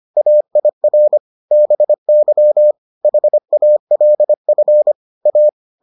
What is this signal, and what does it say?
Morse code, heard as a keyed tone, about 25 wpm, 600 Hz. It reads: AIR BY HALF A